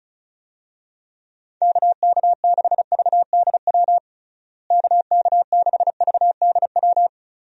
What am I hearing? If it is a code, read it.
Morse code, heard as a keyed tone, about 34 wpm, 700 Hz. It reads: KK6VDW KK6VDW
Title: USA Call Signs - 6 Characters - Repeated 3x - 35wpm